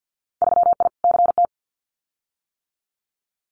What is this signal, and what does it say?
Morse code, heard as a keyed tone, about 50 wpm, 700 Hz. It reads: 3I CT
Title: Winter Field Day Exchanges 50wpm